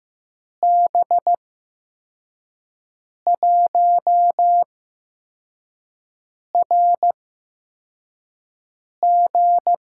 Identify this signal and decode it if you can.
Morse code, heard as a keyed tone, about 15 wpm, spaced as for 4 wpm, 700 Hz. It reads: B1RG
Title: USA Call Signs - 6 Characters - 8x Character Spacing - 15wpm